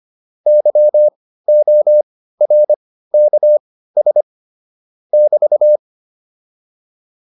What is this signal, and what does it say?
Morse code, heard as a keyed tone, about 25 wpm, spaced as for 15 wpm, 600 Hz. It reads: YORKS <BT>
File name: QSO404 15